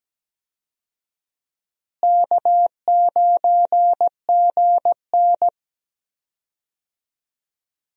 Morse code, heard as a keyed tone, about 17 wpm, 700 Hz. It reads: K9GN
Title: USA Call Signs - 4 Characters - Repeated 3x - 17wpm